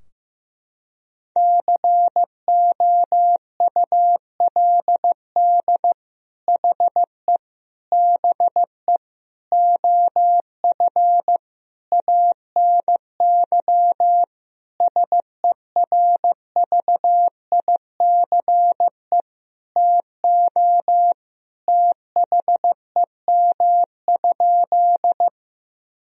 Morse code, heard as a keyed tone, about 15 wpm, 700 Hz. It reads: COULD HE BE OF ANY SERVICE TO THEM?
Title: Sentences from Top 1000 Words - Review Part 4 - 15wpm